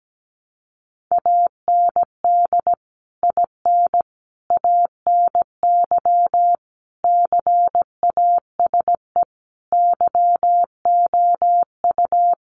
Morse code, heard as a keyed tone, about 17 wpm, 700 Hz. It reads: AND IN ANY CASE YOU
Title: Sentences from Top 1200 Words 17wpm